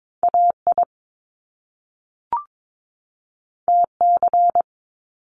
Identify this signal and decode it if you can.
Morse code, heard as a keyed tone, about 22 wpm, 700 Hz. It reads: AI TC
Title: Top Two Letter Combinations in Words 22wpm